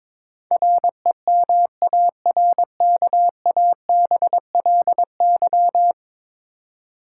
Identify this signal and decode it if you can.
Morse code, heard as a keyed tone, about 21 wpm, 700 Hz. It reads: REMARKABLY
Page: morse-code-ninja-practice-22wpm